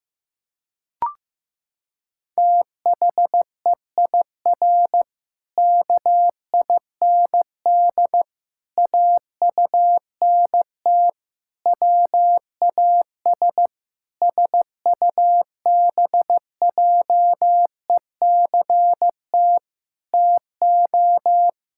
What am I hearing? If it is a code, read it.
Morse code, heard as a keyed tone, about 15 wpm, 700 Hz. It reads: THEIR KIND AUNT WAS SUBJECT TO
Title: Pangram Challenge v2 - 15wpm